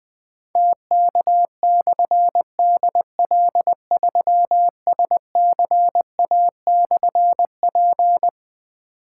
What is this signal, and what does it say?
Morse code, heard as a keyed tone, about 20 wpm, 700 Hz. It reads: TK/DL3SCA/P
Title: Difficult Call Signs 20wpm